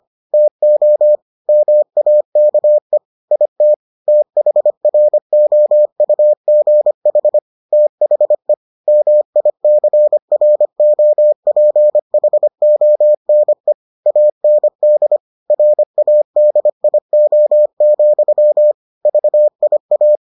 Morse code, heard as a keyed tone, about 25 wpm, 600 Hz. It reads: TO MAKE IT THROUGH THE MICROPHONE AND RADIO, VIA